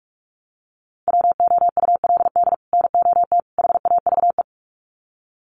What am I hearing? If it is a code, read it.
Morse code, heard as a keyed tone, about 44 wpm, 700 Hz. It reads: WOULD NOT HAVE
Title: Sets of 3 Words 45wpm